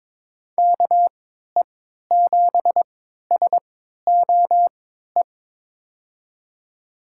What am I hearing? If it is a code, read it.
Morse code, heard as a keyed tone, about 22 wpm, spaced as for 12 wpm, 700 Hz. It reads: KE7SOE
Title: USA Call Signs - 6 Characters - 3x Character Spacing - 22wpm